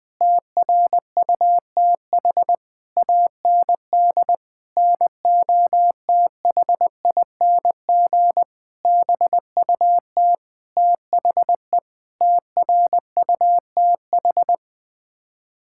Morse code, heard as a keyed tone, about 20 wpm, 700 Hz. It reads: TRUTH AND NOTHING BUT THE TRUTH
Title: Sets of 8 Words - 20wpm